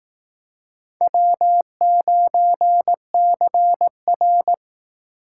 Morse code, heard as a keyed tone, about 18 wpm, 700 Hz. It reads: W9CR
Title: Speed Racing - USA 1x2 Call Signs 15wpm